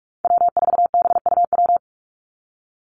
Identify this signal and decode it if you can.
Morse code, heard as a keyed tone, about 44 wpm, 700 Hz. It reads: W4BUW
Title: Call Signs - 45wpm